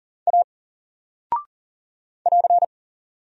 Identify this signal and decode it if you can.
Morse code, heard as a keyed tone, about 40 wpm, 700 Hz. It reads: A <AR>
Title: Single Character (Letter:Number:Punctuation:Prosign) Alt 40wpm